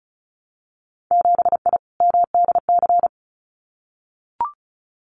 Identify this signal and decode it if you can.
Morse code, heard as a keyed tone, about 36 wpm, 700 Hz. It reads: 7I MDC
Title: Winter Field Day Exchanges Encore 35wpm